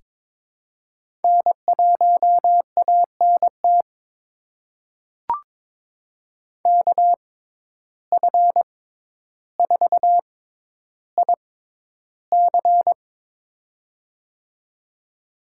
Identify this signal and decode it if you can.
Morse code, heard as a keyed tone, about 22 wpm, spaced as for 11 wpm, 700 Hz. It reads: N1ANT KF4IC
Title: USA Call Signs - 5 Characters - 6x Character Spacing - 22wpm